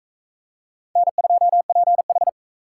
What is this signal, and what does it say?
Morse code, heard as a keyed tone, about 40 wpm, 700 Hz. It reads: N2PH